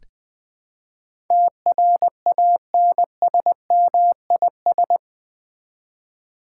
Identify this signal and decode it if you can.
Morse code, heard as a keyed tone, about 20 wpm, 700 Hz. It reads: TRANSMIS
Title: Build A Word - 4 Letters At A Time - 12-Letter Words - 20wpm